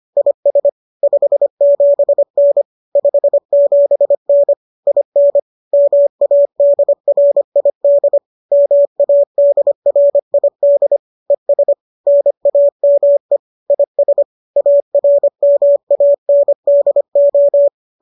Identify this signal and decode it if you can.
Morse code, heard as a keyed tone, about 25 wpm, 570 Hz. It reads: IS 57N 57N IN MADRID MADRID ES NAME IS ARMANDO